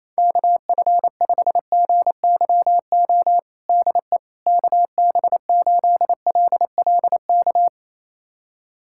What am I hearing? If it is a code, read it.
Morse code, heard as a keyed tone, about 28 wpm, 700 Hz. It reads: KF5GYO DE KB8LLK
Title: USA Callsign DE Callsign 28wpm